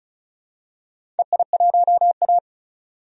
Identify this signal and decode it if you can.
Morse code, heard as a keyed tone, about 36 wpm, 700 Hz. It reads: EI1A
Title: DXCC-100 Call Signs - 4 Characters - 35wpm